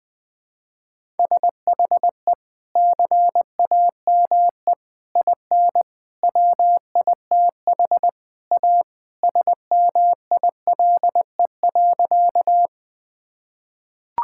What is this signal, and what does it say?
Morse code, heard as a keyed tone, about 20 wpm, 700 Hz. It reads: SHE CAME IN WITH A SMILE.
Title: Sentences with 2-Word Phrases - Spoken Beginning - 20wpm